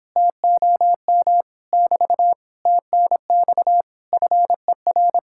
Morse code, heard as a keyed tone, about 29 wpm, 700 Hz. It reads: TOM <BT> TNX FER